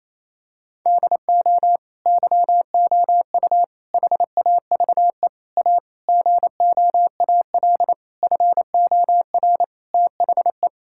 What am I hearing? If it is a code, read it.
Morse code, heard as a keyed tone, about 28 wpm, 700 Hz. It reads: DO YOU HAVE A GOAL FOR THE
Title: Sentences from Top 1000 Words - Review Part 5 - 28wpm